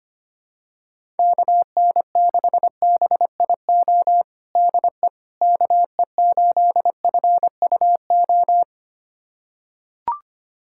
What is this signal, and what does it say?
Morse code, heard as a keyed tone, about 25 wpm, 700 Hz. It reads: KN6BIO DE KE8FUO
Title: USA Callsign DE Callsign - 6 Characters - 25wpm